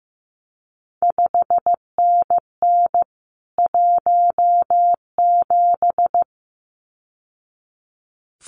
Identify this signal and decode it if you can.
Morse code, heard as a keyed tone, about 15 wpm, 700 Hz. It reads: 5NN 17
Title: CQ WW DX Contest Exchange Encore - 15wpm